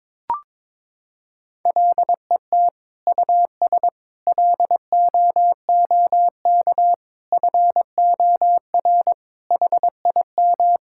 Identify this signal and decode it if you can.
Morse code, heard as a keyed tone, about 22 wpm, 700 Hz. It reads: LET US LOOK FOR HIM
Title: Sentences from Top 1600 Words 22wpm